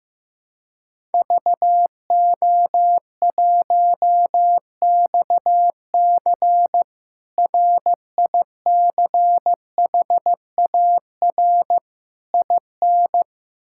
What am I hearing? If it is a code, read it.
Morse code, heard as a keyed tone, about 15 wpm, 700 Hz. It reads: VO1XC RICHAR IN
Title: K1USN SST QSO Practice - 15wpm